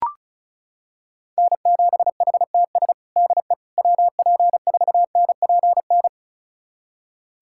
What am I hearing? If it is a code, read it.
Morse code, heard as a keyed tone, about 36 wpm, 700 Hz. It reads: N7HTS DE WP4NPN
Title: USA Callsign DE Callsign 35wpm